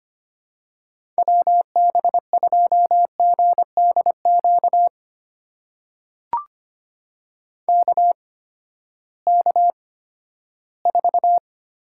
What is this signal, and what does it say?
Morse code, heard as a keyed tone, about 25 wpm, spaced as for 14 wpm, 700 Hz. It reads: WB2GDQ KK4
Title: USA Call Signs - 8x Character Spacing - 25wpm